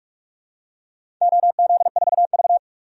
Morse code, heard as a keyed tone, about 47 wpm, 700 Hz. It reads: OZ4V
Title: DXCC-100 Call Signs - 4 Characters - 45wpm